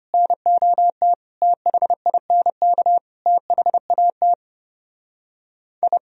Morse code, heard as a keyed tone, about 33 wpm, 700 Hz. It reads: NOT THINK THAT I